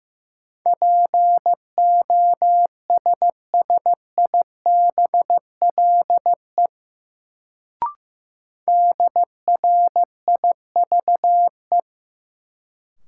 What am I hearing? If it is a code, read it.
Morse code, heard as a keyed tone, about 15 wpm, 700 Hz. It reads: POSSIBLE DRIVE
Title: Sentences from Top 500 Words - New Word Drill - 15wpm